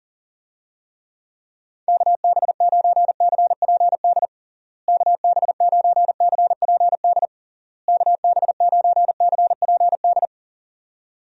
Morse code, heard as a keyed tone, about 40 wpm, 700 Hz. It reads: KB9CPD KB9CPD KB9CPD
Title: USA Call Signs - 6 Characters - Repeated 3x - 40wpm